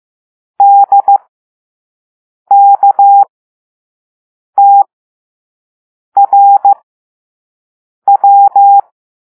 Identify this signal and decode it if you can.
Morse code, heard as a keyed tone, about 15 wpm, spaced as for 5 wpm, 800 Hz. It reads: DKTRW